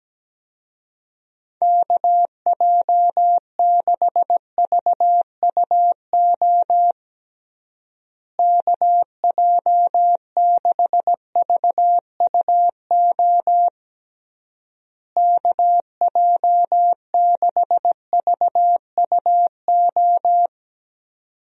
Morse code, heard as a keyed tone, about 17 wpm, 700 Hz. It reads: KJ6VUO KJ6VUO KJ6VUO
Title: USA Call Signs - 6 Characters - Repeated 3x - 17wpm